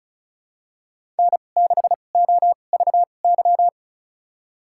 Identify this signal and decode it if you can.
Morse code, heard as a keyed tone, about 36 wpm, spaced as for 25 wpm, 700 Hz. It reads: N6OVY
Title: USA Call Signs - 5 Characters - 2x Character Spacing - 35wpm